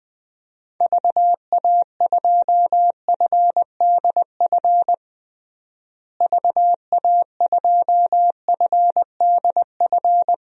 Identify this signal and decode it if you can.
Morse code, heard as a keyed tone, about 20 wpm, 700 Hz. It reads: VA2FDF VA2FDF
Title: DXCC-100 Call Signs - 6 Characters - Repeated 2x - 20wpm